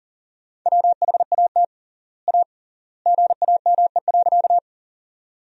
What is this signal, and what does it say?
Morse code, heard as a keyed tone, about 40 wpm, 700 Hz. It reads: WHAT A GAME.
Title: 3x Word-Spacing - Sentences from Top 500 Words - 40wpm